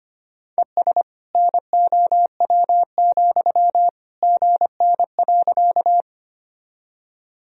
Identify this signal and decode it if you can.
Morse code, heard as a keyed tone, about 25 wpm, 700 Hz. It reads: ES NOW, GN.
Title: Sentences from Top 300 Words and Abbreviations 25wpm